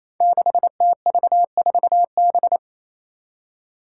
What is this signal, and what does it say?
Morse code, heard as a keyed tone, about 28 wpm, 700 Hz. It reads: 6TV4B
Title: Five Challenging Letter-Number Combinations 28wpm